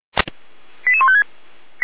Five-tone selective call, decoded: 88156

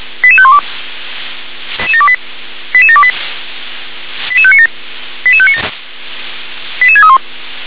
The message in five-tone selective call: 88512 88728 88597 88597 88631

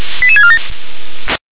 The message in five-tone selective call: 88647

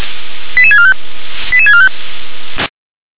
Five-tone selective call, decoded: 88645 88645

